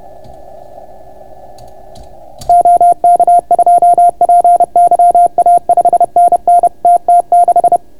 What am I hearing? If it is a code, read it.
Morse code, heard as a keyed tone, about 31 wpm, 690 Hz. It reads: OK2PYA 5NN TT6